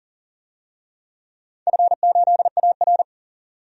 Morse code, heard as a keyed tone, about 40 wpm, 700 Hz. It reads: F8AR
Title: DXCC-100 Call Signs - 4 Characters - 40wpm